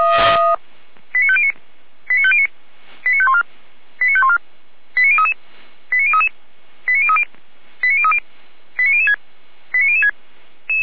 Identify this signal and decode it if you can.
Five-tone selective call (ZVEI), heard as quick runs of five tones, tones of about 70 ms. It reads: 89509 89509 89514 89514 89033 89033 89030 89030 89006 89006
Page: /455/SDR/ZVEI/89099/2022.12.21/